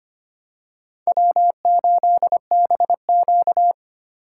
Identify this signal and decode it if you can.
Morse code, heard as a keyed tone, about 25 wpm, 700 Hz. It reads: W8BQ
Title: USA Call Signs - 4 Characters - 3x Character Spacing - 25wpm